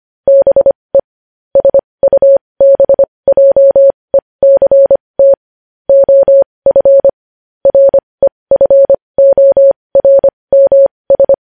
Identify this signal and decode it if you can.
Morse code, heard as a keyed tone, about 25 wpm, spaced as for 20 wpm, 550 Hz. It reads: BE SUBJECT OF REFORMS
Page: news-headlines-in-morse-code-at-20-wpm